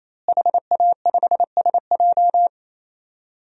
Morse code, heard as a keyed tone, about 28 wpm, 700 Hz. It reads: HA5SJ
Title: DXCC-100 Call Signs - 5 Characters - Repeated 3x - 28wpm